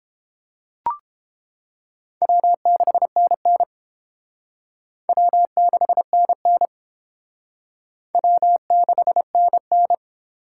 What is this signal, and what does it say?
Morse code, heard as a keyed tone, about 29 wpm, 700 Hz. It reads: W6NN W6NN W6NN